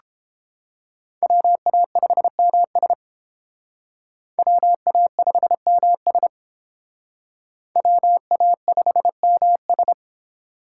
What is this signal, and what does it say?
Morse code, heard as a keyed tone, about 29 wpm, 700 Hz. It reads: WA5MS WA5MS WA5MS